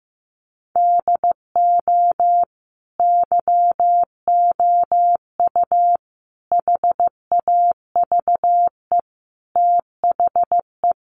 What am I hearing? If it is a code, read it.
Morse code, heard as a keyed tone, about 14 wpm, 700 Hz. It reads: DO YOU HAVE THE
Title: Sentences from Top 1200 Words 15wpm